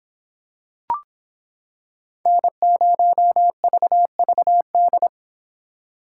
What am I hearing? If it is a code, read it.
Morse code, heard as a keyed tone, about 25 wpm, 700 Hz. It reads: N0VVD